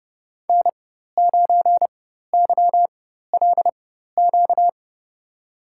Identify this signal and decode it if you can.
Morse code, heard as a keyed tone, about 30 wpm, spaced as for 14 wpm, 700 Hz. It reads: N9YLQ
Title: USA Call Signs - 5 Characters - 4x Character Spacing - 30wpm